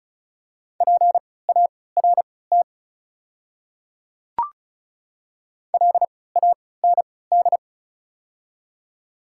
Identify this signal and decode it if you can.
Morse code, heard as a keyed tone, about 36 wpm, spaced as for 20 wpm, 700 Hz. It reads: PART LAND
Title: 3x Character Spacing - Top 200 Words - 35wpm